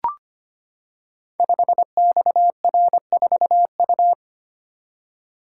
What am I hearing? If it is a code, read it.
Morse code, heard as a keyed tone, about 25 wpm, 700 Hz. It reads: 5XR4U
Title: Code Groups - 5 Letter:Numbers - 25wpm